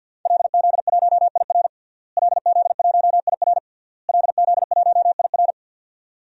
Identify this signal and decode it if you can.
Morse code, heard as a keyed tone, about 51 wpm, 700 Hz. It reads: LZ1IR LZ1IR LZ1IR
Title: DXCC-100 Call Signs - 5 Characters - Repeated 3x - 50wpm